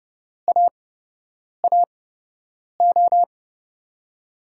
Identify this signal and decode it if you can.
Morse code, heard as a keyed tone, about 32 wpm, spaced as for 8 wpm, 700 Hz. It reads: AAO
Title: USA Call Signs - 5 Characters - 8x Character Spacing - 30wpm